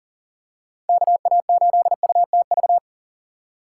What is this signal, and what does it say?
Morse code, heard as a keyed tone, about 40 wpm, 700 Hz. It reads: KA8UTV
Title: USA Call Signs - 6 Characters - 3x Character Spacing - 40wpm